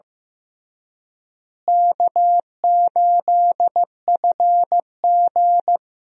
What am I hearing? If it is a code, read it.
Morse code, heard as a keyed tone, about 15 wpm, 700 Hz. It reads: K8FG
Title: USA Call Signs - 4 Characters - Repeated 3x - 15wpm